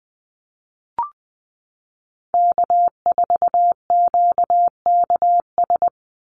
Morse code, heard as a keyed tone, about 20 wpm, 700 Hz. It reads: K4QKS
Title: USA Call Signs 20wpm